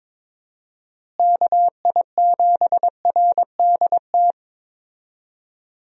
Morse code, heard as a keyed tone, about 22 wpm, 700 Hz. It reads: KI7RDT
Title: USA Call Signs - 6 Characters - Repeated 2x - 22wpm